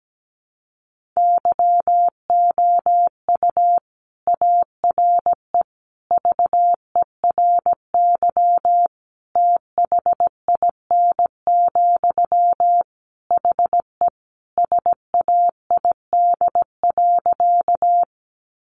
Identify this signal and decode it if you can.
Morse code, heard as a keyed tone, about 17 wpm, 700 Hz. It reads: YOU ARE VERY THIN, HE SAID.
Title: Sentences from Top 1200 Words 17wpm